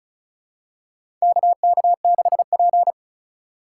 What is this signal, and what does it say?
Morse code, heard as a keyed tone, about 35 wpm, 700 Hz. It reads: KK6P
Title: USA Call Signs - 4 Characters - 4x Character Spacing - 35wpm